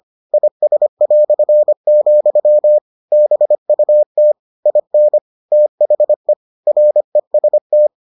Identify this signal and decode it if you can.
Morse code, heard as a keyed tone, about 25 wpm, 600 Hz. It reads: IS", BUT IN THE REST